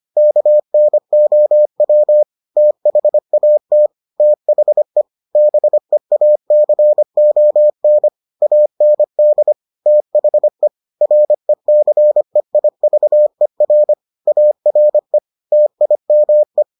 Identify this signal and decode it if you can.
Morse code, heard as a keyed tone, about 25 wpm, 600 Hz. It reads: KNOW THAT THE BEACON AND THE RECEIVER ARE TIME